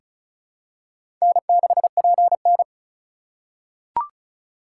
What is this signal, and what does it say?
Morse code, heard as a keyed tone, about 36 wpm, 700 Hz. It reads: N6PN
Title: USA Call Signs - 4 Characters - 2x Character Spacing - 35wpm